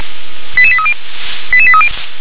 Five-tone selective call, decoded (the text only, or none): 88830 88830